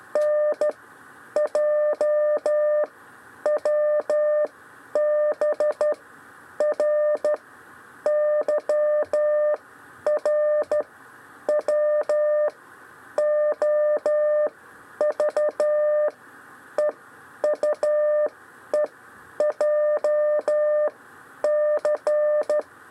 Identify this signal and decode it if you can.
Morse code, heard as a keyed tone, about 13 wpm, spaced as for 9 wpm, 600 Hz. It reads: NJWBR YRWOV EUEJC
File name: 6 lezione cw 1